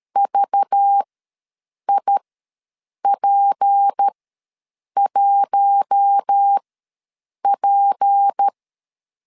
Morse code, heard as a keyed tone, about 13 wpm, spaced as for 7 wpm, 800 Hz. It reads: VIP1P
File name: LEZIONI / 6 PKX0 / 6 P ws 12wpm 3117